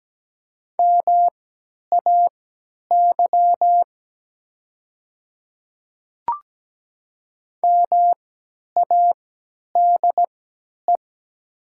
Morse code, heard as a keyed tone, about 17 wpm, spaced as for 10 wpm, 700 Hz. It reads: MAY MADE
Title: 3x Character Spacing - Top 300 Words - 17wpm